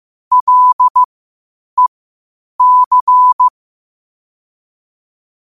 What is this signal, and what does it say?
Morse code, heard as a keyed tone, about 15 wpm, spaced as for 8 wpm, 1000 Hz. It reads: LEC
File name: qotdfiles / 8wpm